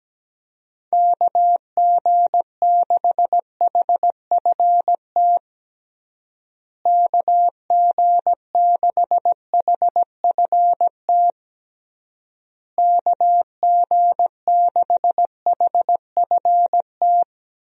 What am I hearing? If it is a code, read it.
Morse code, heard as a keyed tone, about 17 wpm, 700 Hz. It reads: KG6HFT KG6HFT KG6HFT
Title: USA Call Signs - 6 Characters - Repeated 3x - 17wpm